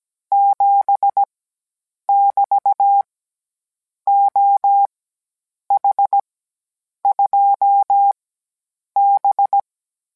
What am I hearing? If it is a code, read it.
Morse code, heard as a keyed tone, about 17 wpm, spaced as for 8 wpm, 800 Hz. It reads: 7 <BT> OH2B